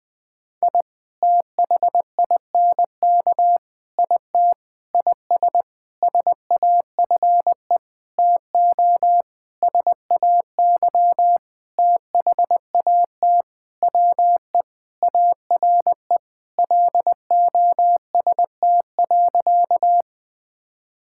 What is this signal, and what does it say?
Morse code, heard as a keyed tone, about 20 wpm, 700 Hz. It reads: I THINK IT IS SAFE TO SAY THAT WE ARE LOST.
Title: Sentences with 7-Word Phrases - Spoken Beginning - 20wpm